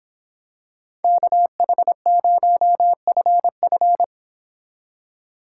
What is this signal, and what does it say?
Morse code, heard as a keyed tone, about 25 wpm, 700 Hz. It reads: KH0FF
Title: Speed Racing - USA 2x2 Call Signs 22wpm